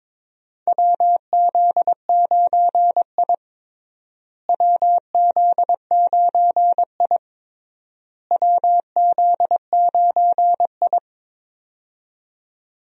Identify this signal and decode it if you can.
Morse code, heard as a keyed tone, about 22 wpm, 700 Hz. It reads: WZ9I WZ9I WZ9I
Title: USA Call Signs - 4 Characters - Repeated 3x - 22wpm